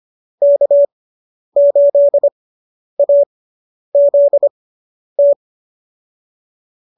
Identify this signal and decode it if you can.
Morse code, heard as a keyed tone, about 25 wpm, spaced as for 10 wpm, 590 Hz. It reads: K8AZT